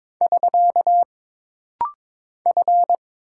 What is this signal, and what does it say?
Morse code, heard as a keyed tone, about 22 wpm, 700 Hz. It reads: <SK> F
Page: This rapid-fire timing is geared to push your ICR (Instant Character Recognition) skills.